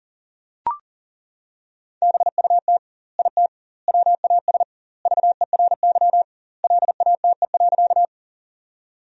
Morse code, heard as a keyed tone, about 40 wpm, 700 Hz. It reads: BUT IT WAS VERY LATE.
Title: 2x Word-Spacing - Sentences from Top 300 Words - 40wpm